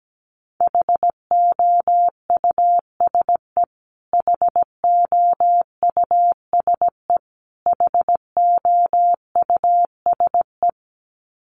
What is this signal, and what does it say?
Morse code, heard as a keyed tone, about 17 wpm, 700 Hz. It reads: HOUSE HOUSE HOUSE
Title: Top 500 Words Repeated 17wpm